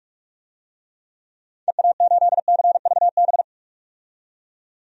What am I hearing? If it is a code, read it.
Morse code, heard as a keyed tone, about 43 wpm, 700 Hz. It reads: EA8CVB
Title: DXCC-100 Call Signs - 6 Characters - 45wpm